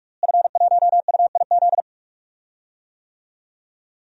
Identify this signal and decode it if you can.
Morse code, heard as a keyed tone, about 47 wpm, 700 Hz. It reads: F1UIZ
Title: DXCC-100 Call Signs - 5 Characters - Repeated 2x - 45wpm